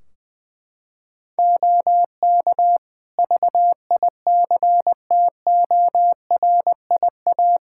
Morse code, heard as a keyed tone, about 20 wpm, 700 Hz. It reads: OK VICTORIA
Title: Practice QSO - Exchange 3 - More Info - 20wpm